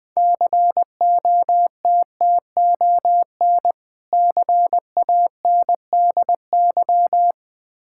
Morse code, heard as a keyed tone, about 21 wpm, 700 Hz. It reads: COTTON CANDY
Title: Difficult Words with Hints - 20wpm